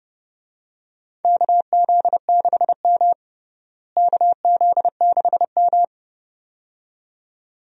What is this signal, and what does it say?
Morse code, heard as a keyed tone, about 30 wpm, 700 Hz. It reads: KZ6M KZ6M
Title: USA Call Signs - 4 Characters - Repeated 2x - 30wpm